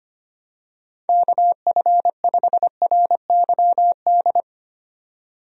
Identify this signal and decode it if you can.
Morse code, heard as a keyed tone, about 25 wpm, 700 Hz. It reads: KF5RYD
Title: USA Call Signs - 6 Characters - 25wpm